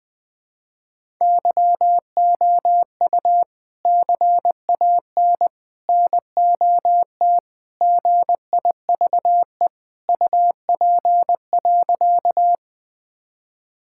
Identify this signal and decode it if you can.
Morse code, heard as a keyed tone, about 20 wpm, 700 Hz. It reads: YOU CAN NOT GIVE UP.
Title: Sentences with 3-Word Phrases - Spoken Beginning - 20wpm